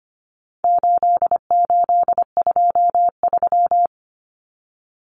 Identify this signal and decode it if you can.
Morse code, heard as a keyed tone, about 25 wpm, 700 Hz. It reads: 8823
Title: Numbers - 4-Digit 25wpm